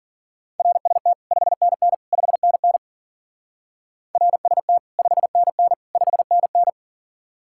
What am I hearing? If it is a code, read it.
Morse code, heard as a keyed tone, about 51 wpm, 700 Hz. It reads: RST 5NN 5NN RST 5NN 5NN